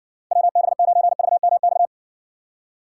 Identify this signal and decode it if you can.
Morse code, heard as a keyed tone, about 60 wpm, 700 Hz. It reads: WB9VMX